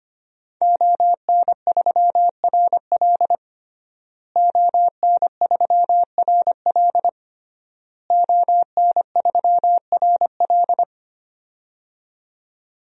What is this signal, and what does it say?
Morse code, heard as a keyed tone, about 25 wpm, 700 Hz. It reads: ON3RL ON3RL ON3RL
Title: DXCC-100 Call Signs - 5 Characters - Repeated 3x - 25wpm